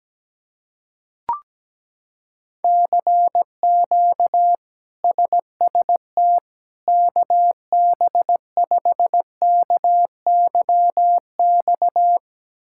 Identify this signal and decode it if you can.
Morse code, heard as a keyed tone, about 17 wpm, 700 Hz. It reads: CQ SST KB5KYX
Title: K1USN SST QSO Practice - 17wpm